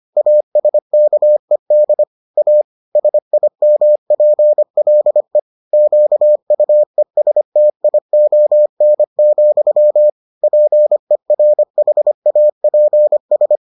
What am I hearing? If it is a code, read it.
Morse code, heard as a keyed tone, about 25 wpm, 600 Hz. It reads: ASKED A SIMPLE QUESTION, PERHAPS